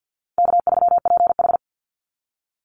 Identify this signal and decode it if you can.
Morse code, heard as a keyed tone, about 48 wpm, 700 Hz. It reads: K3PH
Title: Call Signs Repeated 2x - 50wpm